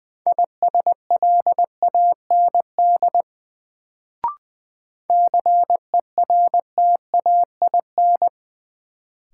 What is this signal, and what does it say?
Morse code, heard as a keyed tone, about 20 wpm, 700 Hz. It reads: ISLAND CERTAIN
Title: Sentences from Top 500 Words - New Word Drill - 20wpm